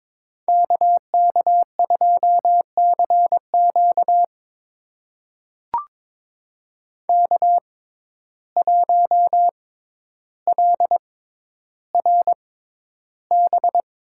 Morse code, heard as a keyed tone, about 22 wpm, spaced as for 11 wpm, 700 Hz. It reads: KK2CQ K1LRB
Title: USA Call Signs - 5 Characters - 6x Character Spacing - 22wpm